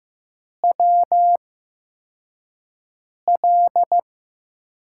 Morse code, heard as a keyed tone, about 15 wpm, spaced as for 4 wpm, 700 Hz. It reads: WL
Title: USA Call Signs - 8x Character Spacing - 15wpm